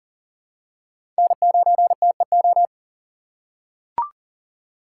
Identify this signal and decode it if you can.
Morse code, heard as a keyed tone, about 40 wpm, 700 Hz. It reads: N9TEO
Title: USA Call Signs - 2x Character Spacing - 40wpm